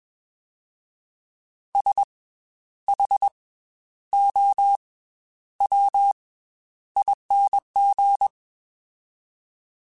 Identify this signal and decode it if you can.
Morse code, heard as a keyed tone, about 21 wpm, 800 Hz. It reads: S H O W ING
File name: Special INGs / Special INGs-mp3s / Slow ing @20